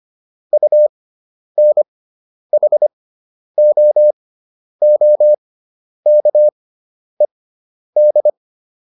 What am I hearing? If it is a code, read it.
Morse code, heard as a keyed tone, about 25 wpm, spaced as for 10 wpm, 600 Hz. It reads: UNHOOKED